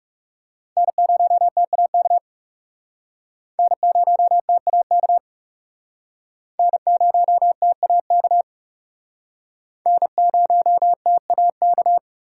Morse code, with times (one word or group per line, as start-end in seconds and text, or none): 0.8-2.2 N0TAK
3.6-5.2 N0TAK
6.6-8.4 N0TAK
9.9-12.0 N0TAK